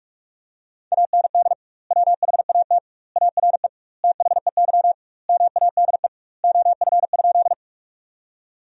Morse code, heard as a keyed tone, about 46 wpm, 700 Hz. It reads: AND WHAT ARE THEY MADE OF?
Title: 2x Word-Spacing - Sentences from Top 200 Words - 45wpm